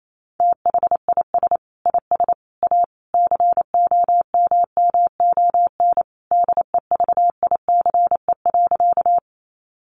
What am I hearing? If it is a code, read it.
Morse code, heard as a keyed tone, about 28 wpm, 700 Hz. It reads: THIS IS A COMMON DEVICE.
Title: Sentences from Top 1500 Words 28wpm